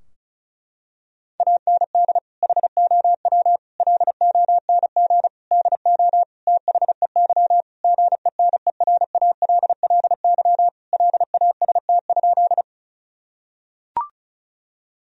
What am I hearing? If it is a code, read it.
Morse code, heard as a keyed tone, about 34 wpm, 700 Hz. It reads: AND HOW LONG DO THEY GENERALLY LAST?
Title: Sentences from Top 1000 Words - Review Part 4 - 35wpm